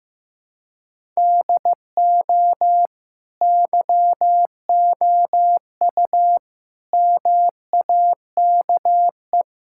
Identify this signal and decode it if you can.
Morse code, heard as a keyed tone, about 15 wpm, 700 Hz. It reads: DO YOU MAKE
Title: Sentences from Top 1000 Words - Review Part 6 - 15wpm